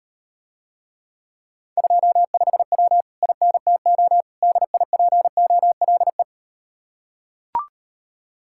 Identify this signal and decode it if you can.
Morse code, heard as a keyed tone, about 38 wpm, 700 Hz. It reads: 25W INTO DIPOLE